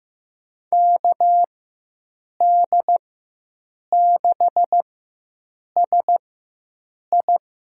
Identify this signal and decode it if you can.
Morse code, heard as a keyed tone, about 15 wpm, spaced as for 7 wpm, 700 Hz. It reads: KD6SI
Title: USA Call Signs - 4x Character Spacing - 15wpm